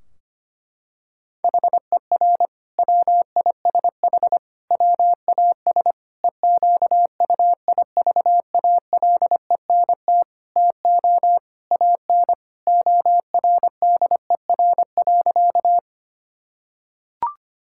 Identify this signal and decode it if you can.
Morse code, heard as a keyed tone, about 25 wpm, 700 Hz. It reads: HER WISH WAS EQUIVALENT TO AN ORDER.
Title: Sentences from Top 1600 Words 25wpm